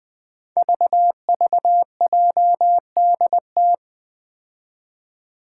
Morse code, heard as a keyed tone, about 20 wpm, 700 Hz. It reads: VVJDT
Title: Five Challenging Letter-Number Combinations 20wpm